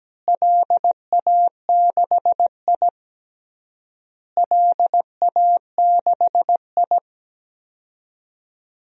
Morse code, heard as a keyed tone, about 17 wpm, 700 Hz. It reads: LA6I LA6I
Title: DXCC-100 Call Signs - 4 Characters - Repeated 3x - 17wpm